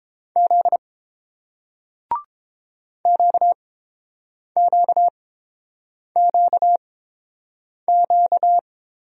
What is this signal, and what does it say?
Morse code, heard as a keyed tone, about 30 wpm, spaced as for 7 wpm, 700 Hz. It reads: Z QQQQ